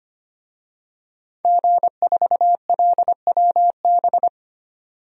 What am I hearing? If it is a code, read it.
Morse code, heard as a keyed tone, about 25 wpm, 700 Hz. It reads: G4LWB
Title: DXCC-100 Call Signs - 5 Characters - 25wpm